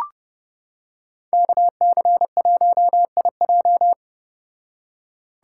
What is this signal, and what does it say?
Morse code, heard as a keyed tone, about 29 wpm, 700 Hz. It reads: KC1IJ